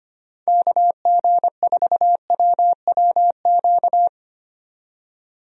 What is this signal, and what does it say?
Morse code, heard as a keyed tone, about 25 wpm, 700 Hz. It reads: KG4WWQ
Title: USA Call Signs - 6 Characters - 25wpm